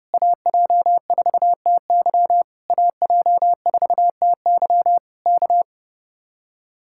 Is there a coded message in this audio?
Yes: Morse code from 0.1 to 5.6 s, heard as a keyed tone, about 32 wpm, 700 Hz. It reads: AJ4TY AJ4TY K